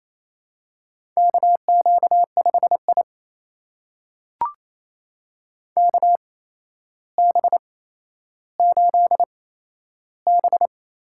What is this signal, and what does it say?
Morse code, heard as a keyed tone, about 27 wpm, spaced as for 12 wpm, 700 Hz. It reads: KQ5I KB8B
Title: USA Call Signs - 8x Character Spacing - 28wpm